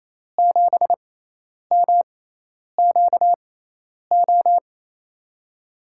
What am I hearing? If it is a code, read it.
Morse code, heard as a keyed tone, about 28 wpm, spaced as for 10 wpm, 700 Hz. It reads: 7MQO
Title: USA Call Signs - 6 Characters - 6x Character Spacing - 28wpm